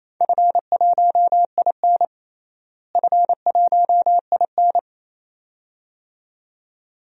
Morse code, heard as a keyed tone, about 28 wpm, 700 Hz. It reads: F1IN F1IN
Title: DXCC-100 Call Signs - 4 Characters - Repeated 2x - 28wpm